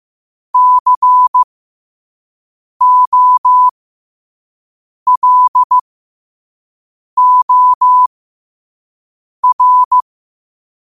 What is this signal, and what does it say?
Morse code, heard as a keyed tone, about 15 wpm, spaced as for 5 wpm, 1000 Hz. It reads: COLOR
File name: qotdfiles / 5wpm